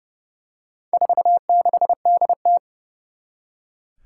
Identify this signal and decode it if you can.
Morse code, heard as a keyed tone, about 30 wpm, 700 Hz. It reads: 46DT
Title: Four Challenging Letter-Number Combinations 30wpm